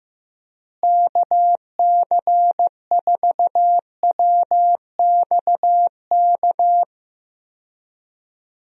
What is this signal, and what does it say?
Morse code, heard as a keyed tone, about 15 wpm, 700 Hz. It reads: KC4WXK
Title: USA Call Signs - 6 Characters - Repeated 3x - 15wpm